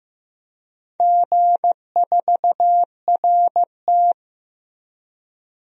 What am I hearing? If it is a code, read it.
Morse code, heard as a keyed tone, about 15 wpm, 700 Hz. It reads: G4RT
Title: DXCC-100 Call Signs - 4 Characters - 15wpm